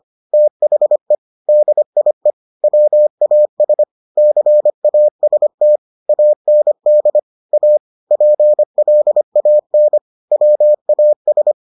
Morse code, heard as a keyed tone, about 25 wpm, 600 Hz. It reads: THE DIE WAS CAST AND A PLAN WAS